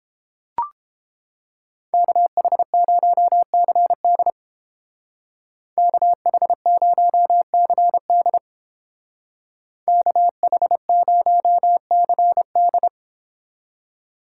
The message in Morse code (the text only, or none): KH0CD KH0CD KH0CD